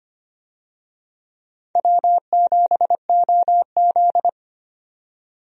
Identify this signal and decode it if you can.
Morse code, heard as a keyed tone, about 25 wpm, 700 Hz. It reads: W7OZ
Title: USA Call Signs - 4 Characters - 25wpm